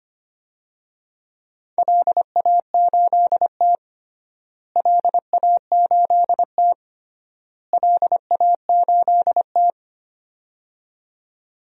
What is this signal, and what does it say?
Morse code, heard as a keyed tone, about 25 wpm, 700 Hz. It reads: LA8T LA8T LA8T
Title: DXCC-100 Call Signs - 4 Characters - Repeated 3x - 25wpm